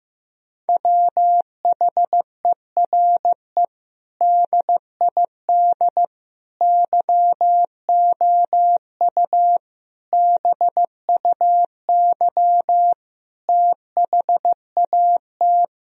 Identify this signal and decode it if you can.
Morse code, heard as a keyed tone, about 16 wpm, 700 Hz. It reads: WHERE DID YOU BUY THAT
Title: Sentences from Top 1000 Words - Review Part 4 - 15wpm